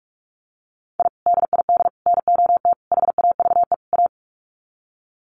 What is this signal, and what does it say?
Morse code, heard as a keyed tone, about 44 wpm, 700 Hz. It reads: I DID NOT HAVE A
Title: Sets of 5 Words 45wpm